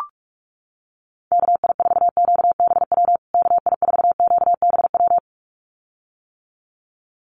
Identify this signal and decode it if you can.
Morse code, heard as a keyed tone, about 44 wpm, 700 Hz. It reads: KI4QBW KI4QBW
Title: USA Call Signs Repeated 2x 45wpm